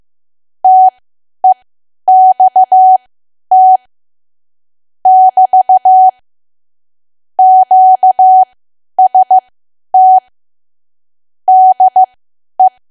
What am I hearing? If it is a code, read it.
Morse code, heard as a keyed tone, about 15 wpm, spaced as for 10 wpm, 750 Hz. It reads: TEXT <BT> QST DE
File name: cwa / W1AW-Level2 / 10WPM